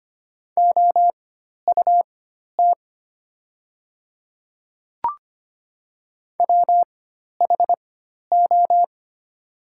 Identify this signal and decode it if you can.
Morse code, heard as a keyed tone, about 25 wpm, spaced as for 12 wpm, 700 Hz. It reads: OUT WHO
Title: 4x Character Spacing - Top 200 Words - 25wpm